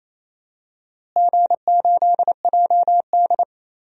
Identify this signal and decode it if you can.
Morse code, heard as a keyed tone, about 28 wpm, 700 Hz. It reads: G8JD
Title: DXCC-100 Call Signs - 4 Characters - Repeated 2x - 28wpm